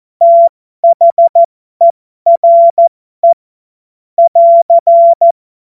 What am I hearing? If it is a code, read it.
Morse code, heard as a keyed tone, about 14 wpm, spaced as for 12 wpm, 680 Hz. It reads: THERE <AR>
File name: Morsekurs / Witze